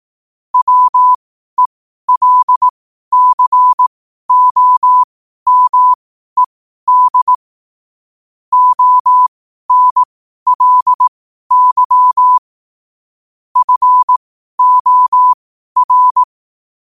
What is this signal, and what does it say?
Morse code, heard as a keyed tone, about 18 wpm, spaced as for 12 wpm, 1000 Hz. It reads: WELCOMED ONLY FOR
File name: qotdfiles / 12wpm